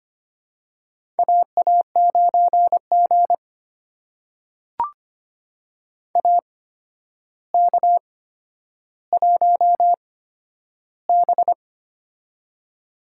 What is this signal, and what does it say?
Morse code, heard as a keyed tone, about 25 wpm, spaced as for 11 wpm, 700 Hz. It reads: AA9G AK1B
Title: USA Call Signs - 4 Characters - 8x Character Spacing - 25wpm